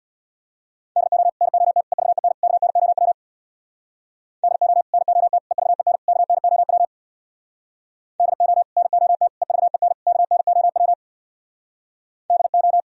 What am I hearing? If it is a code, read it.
Morse code, heard as a keyed tone, about 64 wpm, 700 Hz. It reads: DO NOT EVEN KNOW DO NOT EVEN KNOW DO NOT EVEN KNOW DO